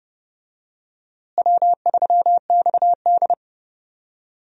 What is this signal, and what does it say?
Morse code, heard as a keyed tone, about 30 wpm, 700 Hz. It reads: W3XD
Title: USA Call Signs - 4 Characters - Repeated 3x - 30wpm